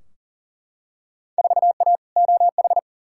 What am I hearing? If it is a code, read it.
Morse code, heard as a keyed tone, about 40 wpm, 700 Hz. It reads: 4A OH
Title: Field Day Exchanges - 40wpm